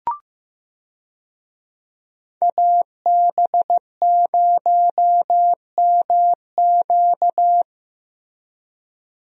Morse code, heard as a keyed tone, about 15 wpm, 700 Hz. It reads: AB0MQ
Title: USA Call Signs - 5 Characters - Repeated 2x - 15wpm